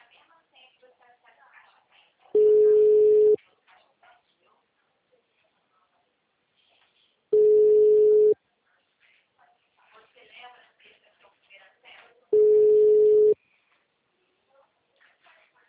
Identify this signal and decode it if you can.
Ringback tone (Europe, 425 Hz), with speech.
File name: out- -2014- - - .33355